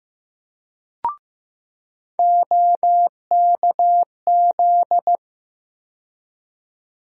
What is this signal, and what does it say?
Morse code, heard as a keyed tone, about 15 wpm, 700 Hz. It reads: OKZ
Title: USA Call Sign Postfixes - 3 Letters 15wpm